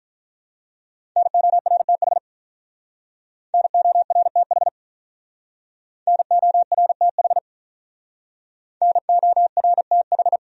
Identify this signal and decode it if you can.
Morse code, heard as a keyed tone, about 48 wpm, 700 Hz. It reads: NORTH NORTH NORTH NORTH